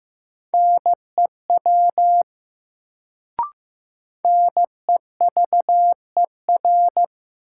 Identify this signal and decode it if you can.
Morse code, heard as a keyed tone, about 15 wpm, 700 Hz. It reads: NEW NEVER
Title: Sentences from Top 300 Words - New Word Drill - 15wpm